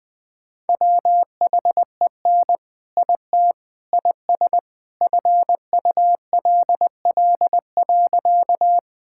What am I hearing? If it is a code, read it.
Morse code, heard as a keyed tone, about 20 wpm, 700 Hz. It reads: WHEN IT IS FULL.
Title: All-Age Appropriate Jokes - 20wpm